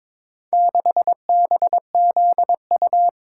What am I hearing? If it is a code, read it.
Morse code, heard as a keyed tone, about 22 wpm, 700 Hz. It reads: 6BZU
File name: practice / 22wpm / Four Letter-Number Combinations - 22wpm